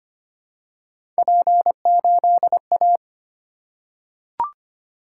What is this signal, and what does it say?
Morse code, heard as a keyed tone, about 25 wpm, 700 Hz. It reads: P8A
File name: Three Letter-Number Combinations - 25wpm